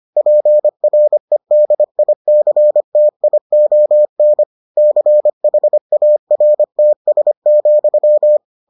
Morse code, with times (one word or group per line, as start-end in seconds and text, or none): 0.2-4.4 PREDICTION
4.8-8.4 CHARTS,